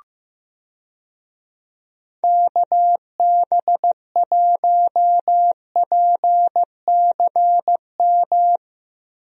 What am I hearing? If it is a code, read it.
Morse code, heard as a keyed tone, about 15 wpm, 700 Hz. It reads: KB1PCM
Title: USA Call Signs - 6 Characters - Repeated 3x - 15wpm